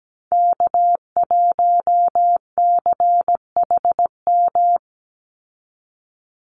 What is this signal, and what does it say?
Morse code, heard as a keyed tone, about 17 wpm, 700 Hz. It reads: K1CHM
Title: Call Signs Repeated 2x - 17wpm